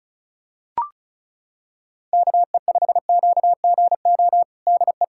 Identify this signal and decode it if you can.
Morse code, heard as a keyed tone, about 34 wpm, 700 Hz. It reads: KE5QGO DE
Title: USA Callsign DE Callsign - 6 Characters - 35wpm